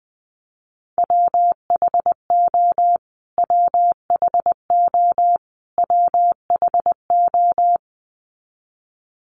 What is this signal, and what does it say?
Morse code, heard as a keyed tone, about 20 wpm, 700 Hz. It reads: WHO WHO WHO
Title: Top 200 Words Repeated 20wpm